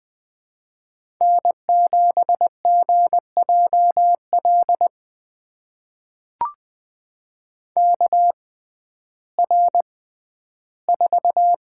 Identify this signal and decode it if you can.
Morse code, heard as a keyed tone, about 20 wpm, spaced as for 12 wpm, 700 Hz. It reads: N7GJL KR4
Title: USA Call Signs - 5 Characters - 6x Character Spacing - 20wpm